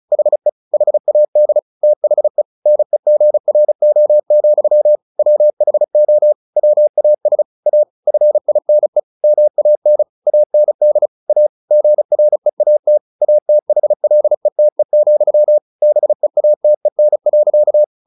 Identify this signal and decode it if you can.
Morse code, heard as a keyed tone, about 35 wpm, 600 Hz. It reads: HE HAD THE NEGRO, WHO WAS A FINE MAN AND A GREAT ATHLETE, BEATEN.
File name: the old man and the sea 2 35wpm